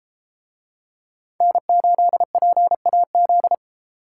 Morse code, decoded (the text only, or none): N8PAZ